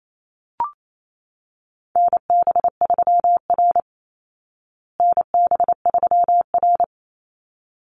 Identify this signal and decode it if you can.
Morse code, heard as a keyed tone, about 28 wpm, 700 Hz. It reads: NB3R NB3R
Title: Call Signs Repeated 3x - 28wpm